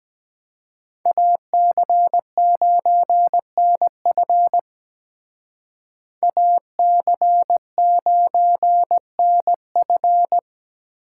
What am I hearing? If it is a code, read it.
Morse code, heard as a keyed tone, about 20 wpm, 700 Hz. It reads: AC9NF AC9NF
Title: Speed Racing - USA Call Signs 17wpm